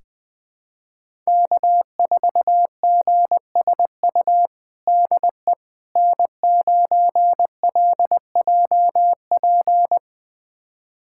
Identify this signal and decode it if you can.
Morse code, heard as a keyed tone, about 20 wpm, 700 Hz. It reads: K4GSU DE N9LJP
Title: USA Callsign DE Callsign 20wpm